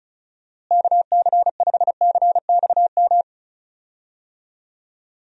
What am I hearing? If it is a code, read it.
Morse code, heard as a keyed tone, about 36 wpm, 700 Hz. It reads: KC5CXM
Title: USA Call Signs - 6 Characters - Repeated 3x - 35wpm